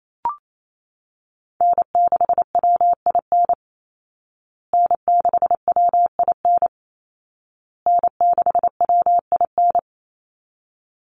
Morse code, heard as a keyed tone, about 28 wpm, 700 Hz. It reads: N6WIN N6WIN N6WIN
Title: Call Signs Repeated 2x - 28wpm